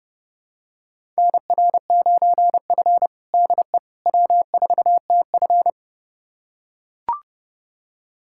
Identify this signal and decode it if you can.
Morse code, heard as a keyed tone, about 30 wpm, 700 Hz. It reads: NR9F DE W4TF
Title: USA Callsign DE Callsign - 4 Characters - 30wpm